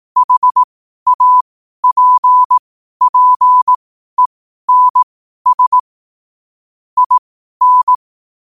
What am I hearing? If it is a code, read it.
Morse code, heard as a keyed tone, about 18 wpm, spaced as for 12 wpm, 1000 Hz. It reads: HAPPENS IN
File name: qotdfiles / 12wpm